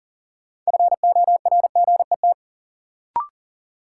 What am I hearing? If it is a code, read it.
Morse code, heard as a keyed tone, about 40 wpm, 700 Hz. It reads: FORGET
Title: Sentences from Top 1200 Words - New Word Drill - 40wpm